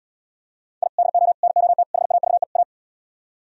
Morse code, heard as a keyed tone, about 77 wpm, 700 Hz. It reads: I DO NOT HAVE A